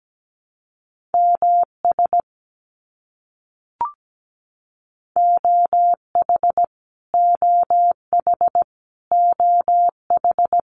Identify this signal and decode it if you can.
Morse code, heard as a keyed tone, about 17 wpm, 700 Hz. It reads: MS OH OH OH
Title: US States Repeated 3x 17wpm